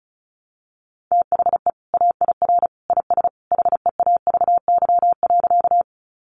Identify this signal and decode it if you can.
Morse code, heard as a keyed tone, about 34 wpm, 700 Hz. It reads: THE AIR IS HEAVY.
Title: Sentences from Top 1100 Words 35wpm